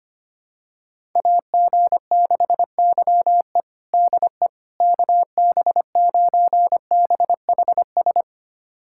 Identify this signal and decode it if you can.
Morse code, heard as a keyed tone, about 25 wpm, 700 Hz. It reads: AG6YE DE KB9BHS
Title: USA Callsign DE Callsign 25wpm